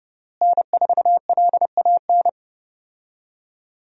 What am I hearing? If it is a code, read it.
Morse code, heard as a keyed tone, about 29 wpm, 700 Hz. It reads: N4LAN
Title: USA Call Signs - 5 Characters - Repeated 2x - 30wpm